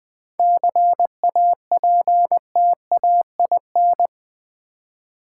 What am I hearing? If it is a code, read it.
Morse code, heard as a keyed tone, about 20 wpm, 690 Hz. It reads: CAPTAIN
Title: 7 Letter Words - First 3 Letters Given - 20wpm